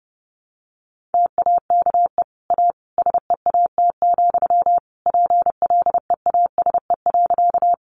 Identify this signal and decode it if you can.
Morse code, heard as a keyed tone, about 30 wpm, 700 Hz. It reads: TAKE A SEAT, PLEASE.
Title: Sentences from Top 1500 Words 30wpm